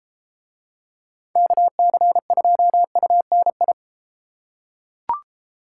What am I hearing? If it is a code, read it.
Morse code, heard as a keyed tone, about 33 wpm, 700 Hz. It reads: KC2UNI